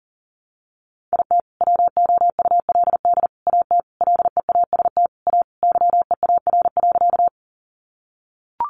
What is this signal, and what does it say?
Morse code, heard as a keyed tone, about 40 wpm, 700 Hz. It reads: IT WOULD AT LEAST A YEAR.
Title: Sentences from Top 700 Words 40wpm